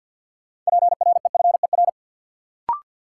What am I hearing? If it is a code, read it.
Morse code, heard as a keyed tone, about 52 wpm, 700 Hz. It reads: PREFER
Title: Sentences from Top 1100 Words - New Word Drill Alt - 50wpm